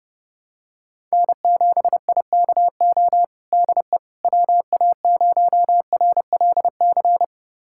Morse code, heard as a keyed tone, about 30 wpm, 700 Hz. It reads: N7IKO DE WA0RLC
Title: USA Callsign DE Callsign 30wpm